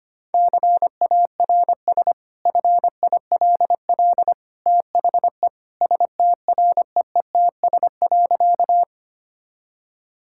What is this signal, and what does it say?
Morse code, heard as a keyed tone, about 25 wpm, 690 Hz. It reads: CARS FILL THE STREETS.
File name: Easy Sentences with Hints - 25wpm